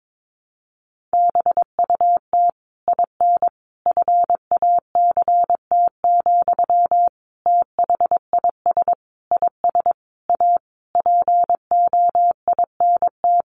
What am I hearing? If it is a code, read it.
Morse code, heard as a keyed tone, about 22 wpm, 700 Hz. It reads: BUT IN FACT, THIS IS A POINT
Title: Sentences from Top 1500 Words 22wpm